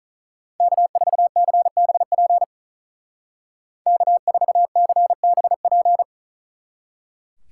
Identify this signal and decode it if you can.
Morse code, heard as a keyed tone, about 44 wpm, 700 Hz. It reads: K4CBP K4CBP
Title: Speed Racing - USA 1x3 Call Signs 35wpm